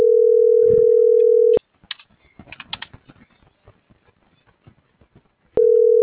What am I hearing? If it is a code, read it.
Ringback tone (North America), 440 + 480 Hz.